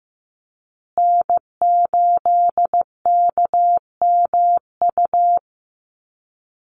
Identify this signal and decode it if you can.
Morse code, heard as a keyed tone, about 15 wpm, 700 Hz. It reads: N8KMU
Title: USA Call Signs Repeated 2x 15wpm